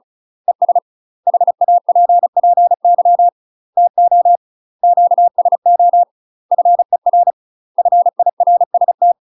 Morse code, heard as a keyed tone, about 34 wpm, 700 Hz. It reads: ES HAPPY TO QSO FER FIRST
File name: UK MASTER QSOs TRAINING / 35WPM / UK MASTER QSOs 0021